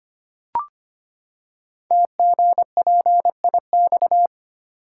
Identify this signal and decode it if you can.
Morse code, heard as a keyed tone, about 25 wpm, 700 Hz. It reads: TGPIX
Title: Code Groups - 5 Letters - 25wpm